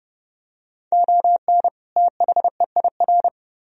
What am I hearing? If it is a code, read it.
Morse code, heard as a keyed tone, about 30 wpm, 700 Hz. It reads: ON THEIR